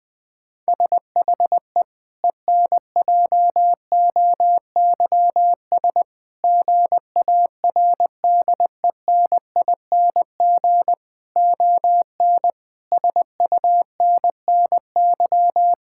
Morse code, heard as a keyed tone, about 19 wpm, 700 Hz. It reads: SHE ENJOYS GARDENING ON SUNNY
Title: Difficult Sentences with Hints - 20wpm